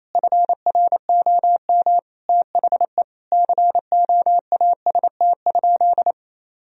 Morse code, heard as a keyed tone, about 28 wpm, 700 Hz. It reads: FROM THE COAST?
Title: Sentences from Top 1000 Words - Review Part 4 - 28wpm